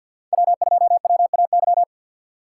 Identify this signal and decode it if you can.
Morse code, heard as a keyed tone, about 52 wpm, 700 Hz. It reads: W2WAY
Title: USA Call Signs - 3x Character Spacing - 50wpm